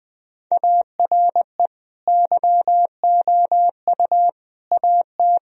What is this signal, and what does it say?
Morse code, heard as a keyed tone, about 19 wpm, 700 Hz. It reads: ARE YOU AT
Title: Sentences from Top 1000 Words - Review Part 2 - 20wpm